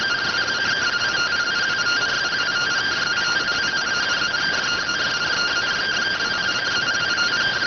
Two-tone teleprinter signal, about 170 Hz shift.